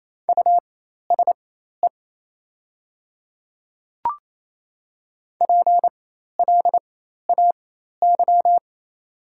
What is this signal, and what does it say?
Morse code, heard as a keyed tone, about 28 wpm, spaced as for 13 wpm, 700 Hz. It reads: USE PLAY
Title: 4x Character Spacing - Top 300 Words - 28wpm